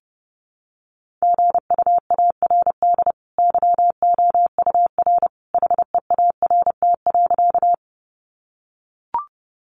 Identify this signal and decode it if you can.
Morse code, heard as a keyed tone, about 28 wpm, 700 Hz. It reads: GUARD YOUR HEART.
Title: Sentences from Top 1200 Words 30wpm